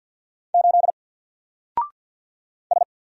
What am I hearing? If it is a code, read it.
Morse code, heard as a keyed tone, about 48 wpm, 700 Hz. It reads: 8 S
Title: Single Letter-Number Alt 50wpm